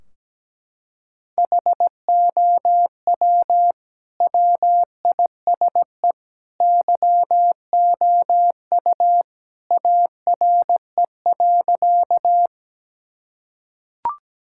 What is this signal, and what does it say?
Morse code, heard as a keyed tone, about 17 wpm, 700 Hz. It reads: HOW WISE YOU ARE.
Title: Sentences from Top 1600 Words 17wpm